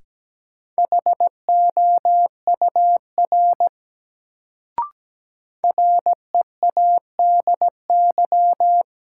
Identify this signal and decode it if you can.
Morse code, heard as a keyed tone, about 17 wpm, 700 Hz. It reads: HOUR READY
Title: Sentences from Top 400 Words - New Word Drill - 17wpm